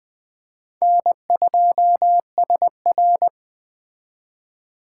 Morse code, heard as a keyed tone, about 20 wpm, 700 Hz. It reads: N2SR
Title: Speed Racing - USA 1x2 Call Signs 17wpm